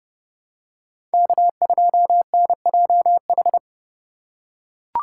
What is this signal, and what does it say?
Morse code, heard as a keyed tone, about 29 wpm, 700 Hz. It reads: K2NJH
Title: USA Call Signs - 2x Character Spacing - 30wpm